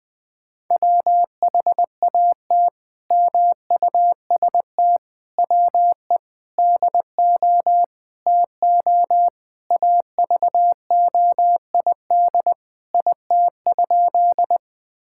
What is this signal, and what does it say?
Morse code, heard as a keyed tone, about 20 wpm, 700 Hz. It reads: WHAT MUST WE DO TO AVOID IT?
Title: Sentences from Top 1000 Words - Review Part 6 - 20wpm